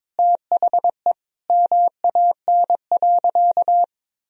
Morse code, heard as a keyed tone, about 22 wpm, 700 Hz. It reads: THE MAN.
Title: Sentences from Top 200 Words and Abbreviations 22wpm